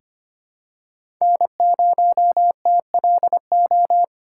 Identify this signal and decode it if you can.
Morse code, heard as a keyed tone, about 25 wpm, 700 Hz. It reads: N0TLO
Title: USA Call Signs - 5 Characters - 2x Character Spacing - 25wpm